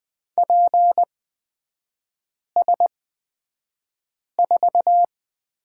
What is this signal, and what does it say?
Morse code, heard as a keyed tone, about 20 wpm, spaced as for 5 wpm, 700 Hz. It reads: PS4
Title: 236 Lesson 38e - Letter q - Three Characters 20wpm